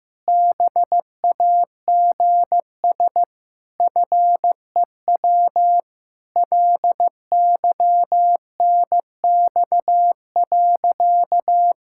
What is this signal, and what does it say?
Morse code, heard as a keyed tone, about 16 wpm, 700 Hz. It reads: BAGS FEW LYNX.
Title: Pangram Challenge v2 - 15wpm